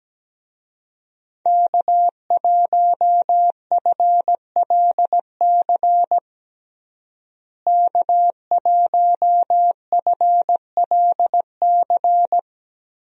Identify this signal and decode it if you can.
Morse code, heard as a keyed tone, about 17 wpm, 700 Hz. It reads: K1FLC K1FLC
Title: USA Call Signs - 5 Characters - Repeated 2x - 17wpm